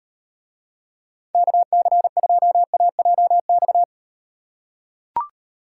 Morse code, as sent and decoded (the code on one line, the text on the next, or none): -.- -.-. ..--- .- .--- -..-
KC2AJX